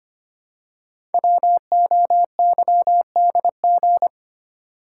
Morse code, heard as a keyed tone, about 25 wpm, 700 Hz. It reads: WOYDG
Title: Code Groups - 5 Letters - 25wpm